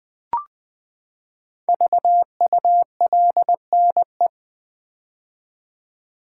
Morse code, heard as a keyed tone, about 20 wpm, 700 Hz. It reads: VULNE
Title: Build A Word - 5 Letters At A Time - 10-Letter Words - 20wpm